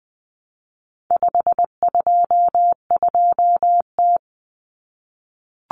Morse code, heard as a keyed tone, about 20 wpm, 700 Hz. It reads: 522T
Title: Common Cut Numbers - 4-Digit 20wpm